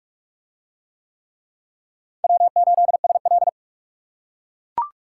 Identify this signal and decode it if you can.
Morse code, heard as a keyed tone, about 39 wpm, 700 Hz. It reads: W8SL